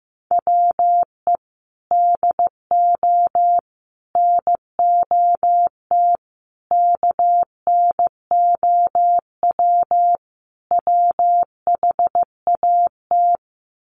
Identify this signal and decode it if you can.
Morse code, heard as a keyed tone, about 15 wpm, 700 Hz. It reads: WE DO NOT KNOW WHAT
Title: Sets of 5 Words 15wpm